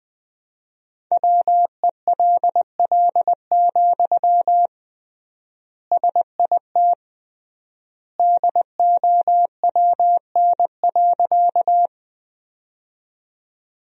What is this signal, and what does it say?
Morse code, heard as a keyed tone, about 20 wpm, 700 Hz. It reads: WELL, SIT DOWN.
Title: 3x Word-Spacing - Sentences from Top 500 Words - 20wpm